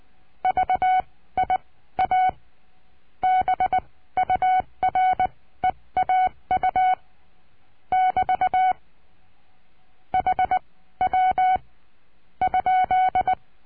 Morse code, heard as a keyed tone, about 20 wpm, spaced as for 15 wpm, 740 Hz. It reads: VIA BUREAU <BT> HW ?
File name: cursos / CURSO DE TELEGRAFIA / Telegrafia - Trilha 22